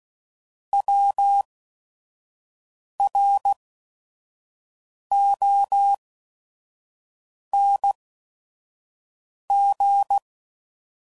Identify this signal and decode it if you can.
Morse code, heard as a keyed tone, about 16 wpm, spaced as for 5 wpm, 800 Hz. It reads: WRONG